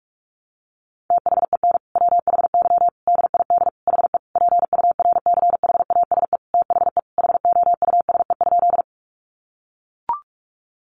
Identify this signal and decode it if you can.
Morse code, heard as a keyed tone, about 44 wpm, 700 Hz. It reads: THEN WHY DID HE PURCHASE THE HOUSE?
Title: Sentences from Top 800 Words 45wpm